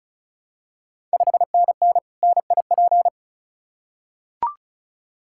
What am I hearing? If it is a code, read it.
Morse code, heard as a keyed tone, about 36 wpm, 700 Hz. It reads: 5NN NIP
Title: Ontario QSO Party Exchange 35wpm